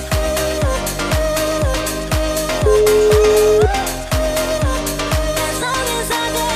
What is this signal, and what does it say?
Ringback tone (Europe, 425 Hz), with noise.